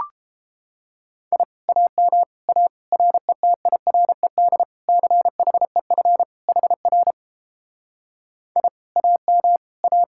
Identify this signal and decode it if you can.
Morse code, heard as a keyed tone, about 32 wpm, 700 Hz. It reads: I AM A RETIRED CHEF HR I AM A